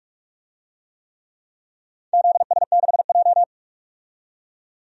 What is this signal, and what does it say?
Morse code, heard as a keyed tone, about 46 wpm, 700 Hz. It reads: ZS6J
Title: DXCC-100 Call Signs - 4 Characters - 45wpm